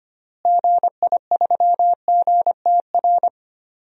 Morse code, heard as a keyed tone, about 25 wpm, 700 Hz. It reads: GI3GTR
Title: DXCC-100 Call Signs - 6 Characters - Repeated 2x - 25wpm